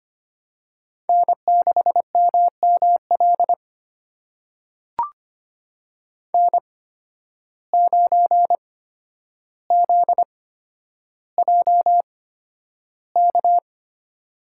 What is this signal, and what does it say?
Morse code, heard as a keyed tone, about 25 wpm, spaced as for 11 wpm, 700 Hz. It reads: N6MML N9ZJK
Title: USA Call Signs - 5 Characters - 8x Character Spacing - 25wpm